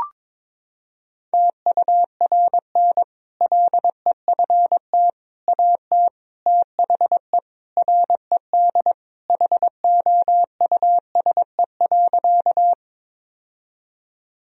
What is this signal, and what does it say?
Morse code, heard as a keyed tone, about 22 wpm, 700 Hz. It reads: TURN LEFT AT THE RED HOUSE.
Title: Sentences from Top 1000 Words - Review Part 2 - 22wpm